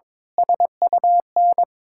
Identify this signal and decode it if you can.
Morse code, heard as a keyed tone, about 22 wpm, 700 Hz. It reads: SUN
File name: Sentences with 7-Word Phrases - 22wpm